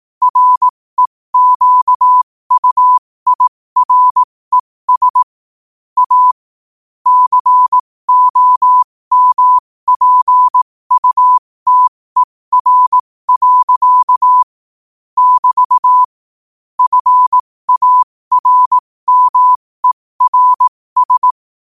Morse code, heard as a keyed tone, about 18 wpm, spaced as for 15 wpm, 1000 Hz. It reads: REQUIRES A COMPUTER. <BT> FARMERS
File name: qotdfiles / 15wpm